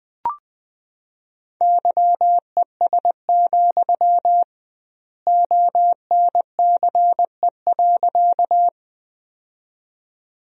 Morse code, heard as a keyed tone, about 20 wpm, 700 Hz. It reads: YES, ONCE.
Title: 2x Word-Spacing - Sentences from Top 500 Words - 20wpm